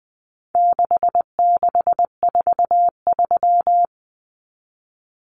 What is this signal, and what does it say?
Morse code, heard as a keyed tone, about 20 wpm, 700 Hz. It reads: 6643
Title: Common Cut Numbers - 4-Digit 20wpm